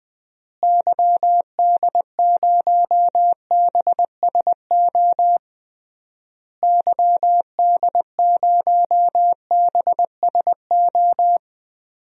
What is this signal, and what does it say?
Morse code, heard as a keyed tone, about 20 wpm, 700 Hz. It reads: YD0BSO YD0BSO
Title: DXCC-100 Call Signs - 6 Characters - Repeated 3x - 20wpm